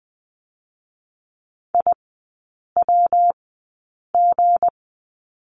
Morse code, heard as a keyed tone, about 20 wpm, spaced as for 8 wpm, 690 Hz. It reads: IWG
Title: 159 Lesson 26e - Letter g - Three Characters 20wpm